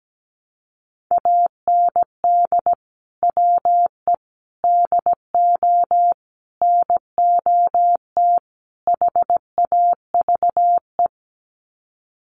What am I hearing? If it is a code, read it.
Morse code, heard as a keyed tone, about 17 wpm, 700 Hz. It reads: AND WE DO NOT HAVE
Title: Sets of 5 Words Encore 17wpm